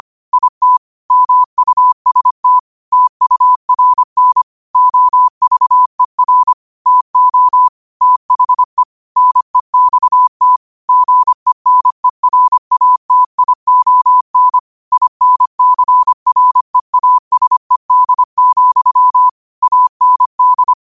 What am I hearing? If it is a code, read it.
Morse code, heard as a keyed tone, about 25 wpm, 1000 Hz. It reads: IT MUST TURN OVER TO THE NEXT GENERATION INCREASED, AND